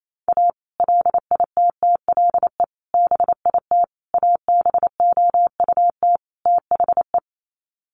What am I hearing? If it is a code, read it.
Morse code, heard as a keyed tone, about 28 wpm, 700 Hz. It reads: A LITTLE BIT ABOUT THE
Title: Sets of 5 Words Encore2 28wpm